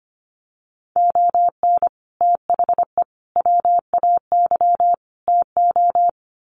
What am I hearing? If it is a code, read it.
Morse code, heard as a keyed tone, about 25 wpm, 700 Hz. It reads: ON THE WAY TO
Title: Sets of 4 Words 25wpm